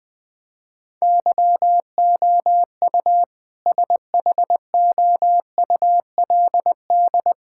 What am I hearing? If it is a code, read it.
Morse code, heard as a keyed tone, about 21 wpm, 700 Hz. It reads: YOU SHOULD
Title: Sentences from Top 1600 Words 20wpm